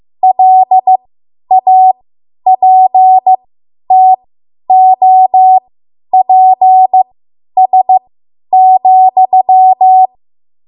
Morse code, heard as a keyed tone, about 15 wpm, spaced as for 10 wpm, 750 Hz. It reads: LAPTOPS,
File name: Morse / 10WPM